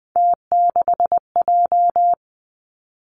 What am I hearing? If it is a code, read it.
Morse code, heard as a keyed tone, about 20 wpm, 700 Hz. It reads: T6J
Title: Three Challenging Letter-Number Combinations 20wpm